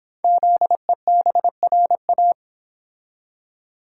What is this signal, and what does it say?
Morse code, heard as a keyed tone, about 26 wpm, 700 Hz. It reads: ZEBRA